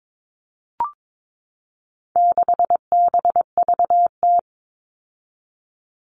Morse code, heard as a keyed tone, about 22 wpm, 700 Hz. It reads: 6BVT
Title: All Cut Numbers - 4-Digit 22wpm